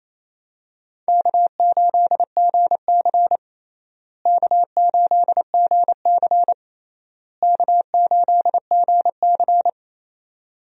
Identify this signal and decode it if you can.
Morse code, heard as a keyed tone, about 28 wpm, 700 Hz. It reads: K8GC K8GC K8GC
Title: USA Call Signs - 4 Characters - Repeated 3x - 28wpm